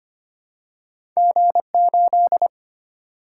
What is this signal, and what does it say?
Morse code, heard as a keyed tone, about 25 wpm, 700 Hz. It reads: G8
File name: Two Letter-Number Combinations - 25wpm